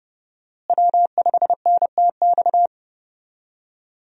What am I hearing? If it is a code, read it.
Morse code, heard as a keyed tone, about 30 wpm, 700 Hz. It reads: W5NTX
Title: USA Call Signs - 5 Characters - Repeated 2x - 30wpm